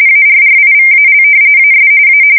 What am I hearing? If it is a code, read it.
RTTY, about 45.45 baud. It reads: ARRIVAL AS A
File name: fsk170 / battleshipnj